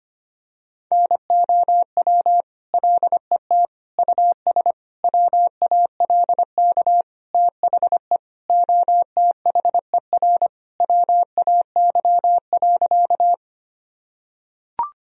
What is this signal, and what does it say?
Morse code, heard as a keyed tone, about 25 wpm, 700 Hz. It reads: NOW LET US WALK THE OTHER WAY.
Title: Sentences from Top 1000 Words - Review Part 1 - 25wpm